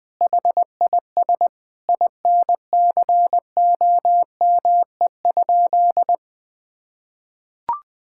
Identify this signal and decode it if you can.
Morse code, heard as a keyed tone, about 21 wpm, 700 Hz. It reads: HIS INCOME?
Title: Sentences from Top 1000 Words - Review Part 5 - 20wpm